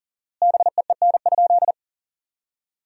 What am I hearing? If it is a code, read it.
Morse code, heard as a keyed tone, about 38 wpm, 700 Hz. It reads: BEEN?
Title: 4x Word-Spacing - Sentences from Top 200 Words - 40wpm